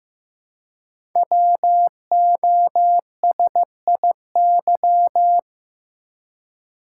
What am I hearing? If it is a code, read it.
Morse code, heard as a keyed tone, about 15 wpm, 700 Hz. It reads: WOSIY
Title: Code Groups - 5 Letter:Numbers - 15wpm